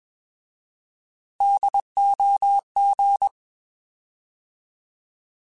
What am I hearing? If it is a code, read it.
Morse code, heard as a keyed tone, about 21 wpm, 800 Hz. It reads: DOG
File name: Alphabetical / Alphabetical mp3s / Ds @20